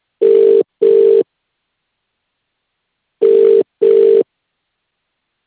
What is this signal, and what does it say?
Ringback tone (UK), 400 + 450 Hz.